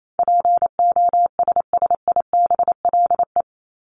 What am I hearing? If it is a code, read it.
Morse code, heard as a keyed tone, about 28 wpm, 700 Hz. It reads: POSSIBLE
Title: Top 500 Words 28wpm